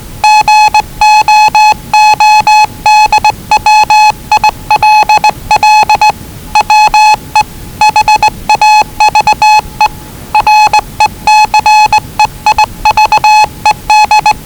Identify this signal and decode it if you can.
Morse code, heard as a keyed tone, about 20 wpm, 840 Hz. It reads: GOODWILL WE HAVE RECEIVED
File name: WellingtonRadioZlwCloseDownOnHfAndMf / 01-Zlw 4277khz 417.5 qrt until 0925z